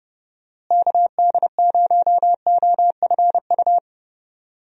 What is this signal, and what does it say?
Morse code, heard as a keyed tone, about 31 wpm, 700 Hz. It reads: KD0OFU
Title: USA Call Signs - 6 Characters - Repeated 2x - 30wpm